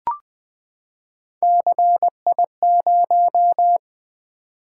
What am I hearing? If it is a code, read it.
Morse code, heard as a keyed tone, about 20 wpm, 700 Hz. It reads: CI0
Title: Three Letter-Number Combinations - 20wpm